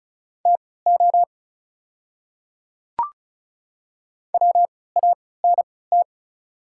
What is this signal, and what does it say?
Morse code, heard as a keyed tone, about 34 wpm, spaced as for 20 wpm, 700 Hz. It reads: TO WANT
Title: 3x Character Spacing - Top 200 Words - 35wpm